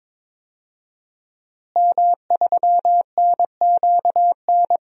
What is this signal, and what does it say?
Morse code, heard as a keyed tone, about 22 wpm, 700 Hz. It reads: M3NQN
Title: DXCC-100 Call Signs - 5 Characters - Repeated 2x - 22wpm